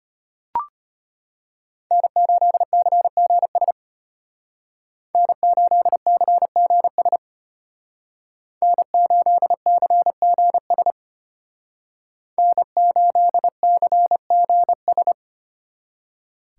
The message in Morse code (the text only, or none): N8CGS N8CGS N8CGS N8CGS